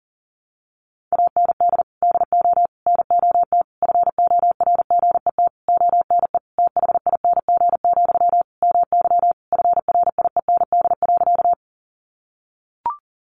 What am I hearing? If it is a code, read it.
Morse code, heard as a keyed tone, about 37 wpm, 700 Hz. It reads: AND DO NOT FORGET ONE THING, MY FRIEND.
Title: Sentences from Top 1200 Words 40wpm